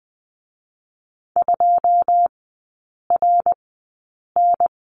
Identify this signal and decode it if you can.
Morse code, heard as a keyed tone, about 20 wpm, spaced as for 8 wpm, 690 Hz. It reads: 2RN
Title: 087 Lesson 15e - Number 5 - Three Characters 20wpm